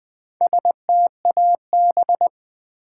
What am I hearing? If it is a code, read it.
Morse code, heard as a keyed tone, about 20 wpm, 700 Hz. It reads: STAB
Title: Build A Word - 2 Letters At A Time - 6-Letter Words - 20wpm